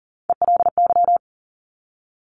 Morse code, heard as a keyed tone, about 40 wpm, 700 Hz. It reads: ELY
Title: Top Three Letter Combinations in Words 40wpm